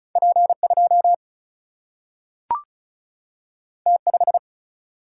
Morse code, heard as a keyed tone, about 36 wpm, 700 Hz. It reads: P 2 T 5
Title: Single-Letter Single-Number 35wpm